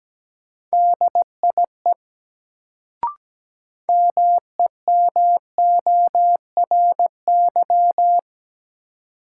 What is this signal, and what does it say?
Morse code, heard as a keyed tone, about 17 wpm, 700 Hz. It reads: DIE MEMORY
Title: Sentences from Top 1300 Words - New Word Drill Alt - 17wpm